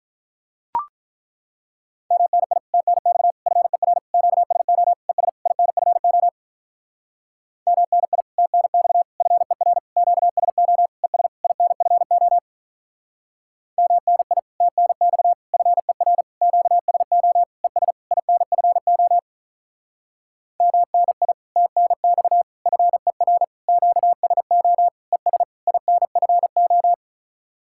Morse code, heard as a keyed tone, about 42 wpm, 700 Hz. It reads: MNI TNX FER QSO ES INFO MNI TNX FER QSO ES INFO MNI TNX FER QSO ES INFO MNI TNX FER QSO ES INFO
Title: Speed-Racing Common QSO Phrases Encore - 35wpm